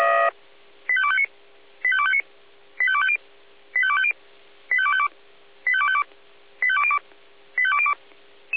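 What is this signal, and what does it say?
Five-tone selective call (ZVEI), heard as quick runs of five tones, tones of about 70 ms. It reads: 85259 85259 85250 85250 85252 85252 85202 85202
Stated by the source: ZVEI five-tone call received by SDR, feuerwehr-orlamuende.org archive.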